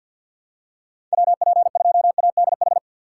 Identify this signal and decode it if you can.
Morse code, heard as a keyed tone, about 53 wpm, 700 Hz. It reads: WP2ADH
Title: USA Call Signs - 6 Characters - 3x Character Spacing - 50wpm